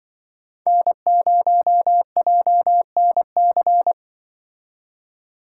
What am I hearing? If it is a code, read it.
Morse code, heard as a keyed tone, about 24 wpm, 700 Hz. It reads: N0JNC
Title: Speed Racing - USA Call Signs 20wpm